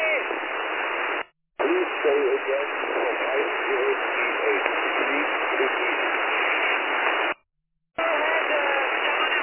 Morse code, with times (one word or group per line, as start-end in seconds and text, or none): none